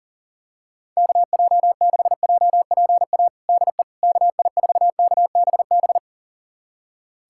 Morse code, heard as a keyed tone, about 40 wpm, 700 Hz. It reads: KJ6JPA DE KI4KBB
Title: USA Callsign DE Callsign - 6 Characters - 40wpm